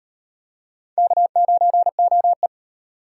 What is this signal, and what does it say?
Morse code, heard as a keyed tone, about 38 wpm, 700 Hz. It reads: K9OE